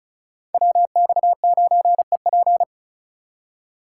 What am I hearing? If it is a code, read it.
Morse code, heard as a keyed tone, about 34 wpm, 700 Hz. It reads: WX9EP
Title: USA Call Signs - 4x Character Spacing - 35wpm